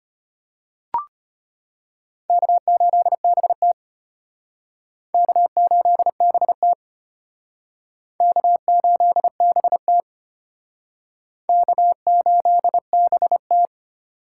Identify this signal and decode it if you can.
Morse code, heard as a keyed tone, about 31 wpm, 700 Hz. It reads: K8BT K8BT K8BT K8BT
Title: Speed Racing - USA 1x2 Call Signs 25wpm